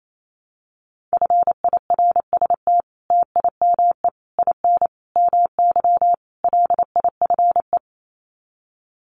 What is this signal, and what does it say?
Morse code, heard as a keyed tone, about 28 wpm, 700 Hz. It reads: FIRST TIME IN MY LIFE
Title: Sets of 5 Words Encore2 28wpm